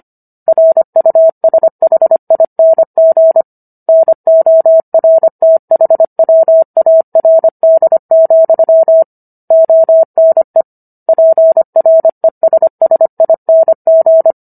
Morse code, heard as a keyed tone, about 25 wpm, 650 Hz. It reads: RUSHING NORTHWARD, ONE PRESSING